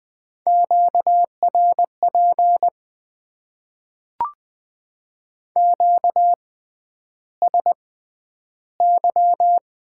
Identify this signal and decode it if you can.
Morse code, heard as a keyed tone, about 20 wpm, spaced as for 10 wpm, 700 Hz. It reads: QRP QSY
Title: Top 90 QSO Elements - 6x Character Spacing - 20wpm